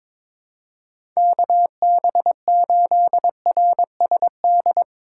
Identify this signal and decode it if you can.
Morse code, heard as a keyed tone, about 21 wpm, 700 Hz. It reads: KB8RSD
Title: USA Call Signs - 6 Characters - 2x Character Spacing - 22wpm